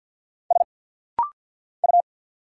Morse code, heard as a keyed tone, about 49 wpm, 700 Hz. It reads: S U
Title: Single Letter-Number Alt - Rapid-Fire 50wpm